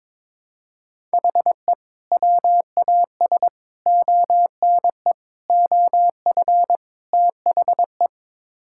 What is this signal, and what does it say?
Morse code, heard as a keyed tone, about 22 wpm, 700 Hz. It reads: HE WAS ONE OF THE
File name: practice / 22wpm / Build A Phrase - 5-Word Phrases - 22wpm